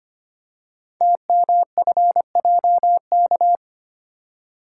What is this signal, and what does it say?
Morse code, heard as a keyed tone, about 25 wpm, 700 Hz. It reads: TMFJK
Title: Code Groups - 5 Letter:Numbers - 25wpm